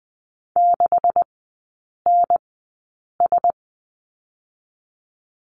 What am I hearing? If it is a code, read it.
Morse code, heard as a keyed tone, about 20 wpm, spaced as for 8 wpm, 700 Hz. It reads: 6NS
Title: 127 Lesson 21e - Number 6 - Three Characters 20wpm